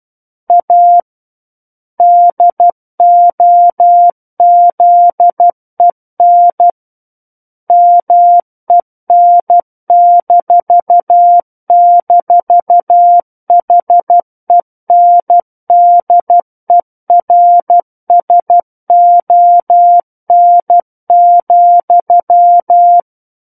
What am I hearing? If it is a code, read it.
Morse code, heard as a keyed tone, about 12 wpm, 700 Hz. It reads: A DOZEN MEN--HENDERSON,
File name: Book One / 03-Chapter 3 - 12 WPM